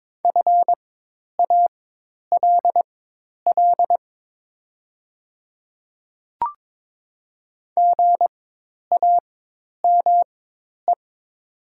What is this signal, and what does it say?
Morse code, heard as a keyed tone, about 22 wpm, spaced as for 10 wpm, 700 Hz. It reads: FALL GAME
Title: 4x Character Spacing - Top 500 Words - 22wpm